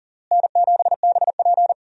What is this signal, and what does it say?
Morse code, heard as a keyed tone, about 40 wpm, 700 Hz. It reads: N7BP
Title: USA Call Signs - 4 Characters - Repeated 2x - 40wpm